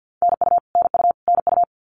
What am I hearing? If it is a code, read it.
Morse code, heard as a keyed tone, about 50 wpm, 700 Hz. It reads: NU NU NU
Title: Canadian Provinces and Territories - Repeated 3x - 50wpm